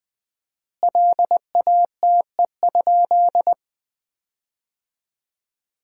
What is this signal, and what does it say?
Morse code, heard as a keyed tone, about 21 wpm, 700 Hz. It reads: LATE?
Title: 4x Word-Spacing - Sentences from Top 300 Words - 20wpm